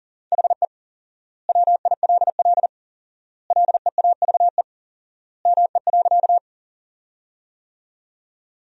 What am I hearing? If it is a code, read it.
Morse code, heard as a keyed tone, about 37 wpm, 700 Hz. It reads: HE WILL LEAVE ME.
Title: 4x Word-Spacing - Sentences from Top 400 Words - 40wpm